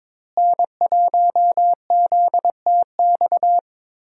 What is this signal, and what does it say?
Morse code, heard as a keyed tone, about 22 wpm, 690 Hz. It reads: N1ZTX
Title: USA Call Signs - 5 Characters - Repeated 3x - 22wpm